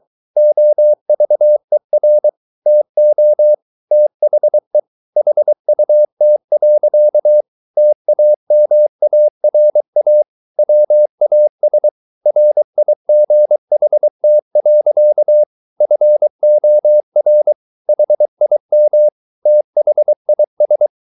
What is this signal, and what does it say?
Morse code, heard as a keyed tone, about 23 wpm, 600 Hz. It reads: OVER TO THE HUT. TAMARA WAS RIGHT. FOR HIM THIS